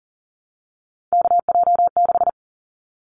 Morse code, heard as a keyed tone, about 40 wpm, 700 Hz. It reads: KJ6
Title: Callsign Prefixes for Top 25 DXCC - Repeated 3x - 40wpm